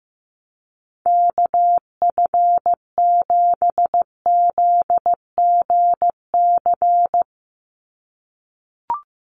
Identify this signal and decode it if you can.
Morse code, heard as a keyed tone, about 15 wpm, 700 Hz. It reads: KF7ZGC
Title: USA Call Signs 15wpm